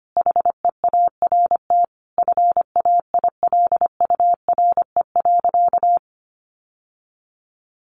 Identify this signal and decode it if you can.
Morse code, heard as a keyed tone, about 25 wpm, 700 Hz. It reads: HEART FAILURE.
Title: Sentences from Top 1300 Words 25wpm